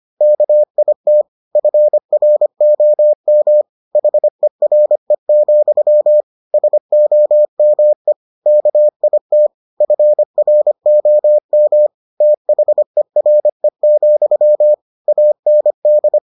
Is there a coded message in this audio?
Yes: Morse code from 0.2 to 16.2 s, heard as a keyed tone, about 25 wpm, 600 Hz. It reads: KIT FROM HERE, SOME KIT FROM THERE, AND